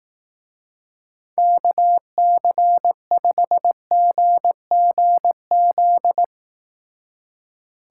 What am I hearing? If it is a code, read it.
Morse code, heard as a keyed tone, about 18 wpm, 700 Hz. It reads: KC5GGZ
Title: Speed Racing - USA 2x3 Call Signs 15wpm